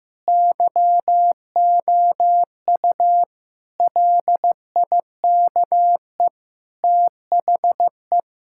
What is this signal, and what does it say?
Morse code, heard as a keyed tone, about 15 wpm, 700 Hz. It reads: YOU LIKE THE
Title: Sentences from Top 1000 Words - Review Part 6 - 15wpm